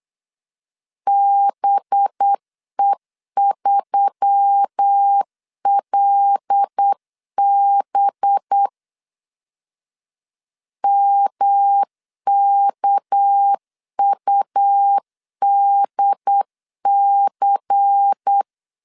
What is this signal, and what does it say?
Morse code, heard as a keyed tone, about 8 wpm, 800 Hz. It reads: BE3LB MKUDC
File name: LEZIONI / 6 PKX0 / 6 PK 8wpm 3003